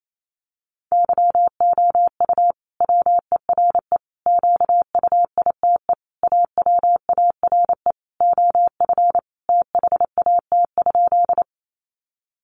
Morse code, heard as a keyed tone, about 28 wpm, 700 Hz. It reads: YOU WERE QUITE AWARE OF THAT?
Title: Sentences from Top 1200 Words 28wpm